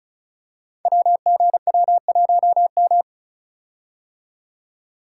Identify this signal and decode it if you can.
Morse code, heard as a keyed tone, about 36 wpm, 700 Hz. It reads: WGW1M
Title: Code Groups - 5 Letter:Numbers - 35wpm